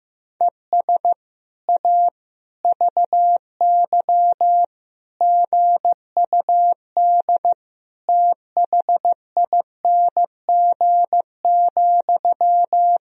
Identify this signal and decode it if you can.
Morse code, heard as a keyed tone, about 15 wpm, 700 Hz. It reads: ES A VY GUD THING,
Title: Sentences from Top 200 Words and Abbreviations 15wpm